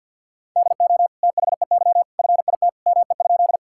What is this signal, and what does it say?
Morse code, heard as a keyed tone, about 50 wpm, 700 Hz. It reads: DO THEY FIT ME?
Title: Sentences from Top 1000 Words - Review Part 5 - 50wpm